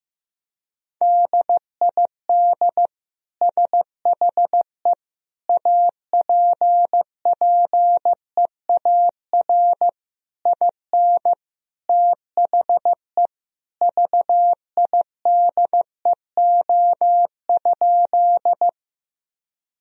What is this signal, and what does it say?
Morse code, heard as a keyed tone, about 16 wpm, 700 Hz. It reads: DID SHE APPEAR IN THE VIDEO?
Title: Sentences from Top 1000 Words - Review Part 4 - 15wpm